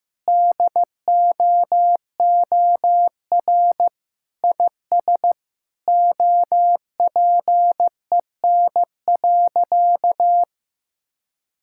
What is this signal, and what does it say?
Morse code, heard as a keyed tone, about 15 wpm, 700 Hz. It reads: DOOR IS OPEN.
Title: Sentences from Top 1600 Words 15wpm